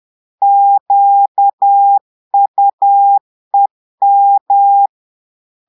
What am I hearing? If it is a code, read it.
Morse code, heard as a keyed tone, about 10 wpm, 800 Hz. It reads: QUEM
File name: Portugais - texte en clair 1 - vitesse - 10 mots minute